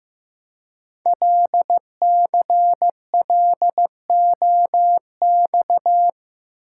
Morse code, heard as a keyed tone, about 15 wpm, 700 Hz. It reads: LCLOX
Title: Code Groups - 5 Letter:Numbers - 15wpm